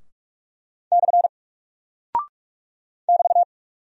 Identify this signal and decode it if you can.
Morse code, heard as a keyed tone, about 44 wpm, 700 Hz. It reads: / <BT>
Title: Single Character (Letter:Number:Punctuation:Prosign) 45wpm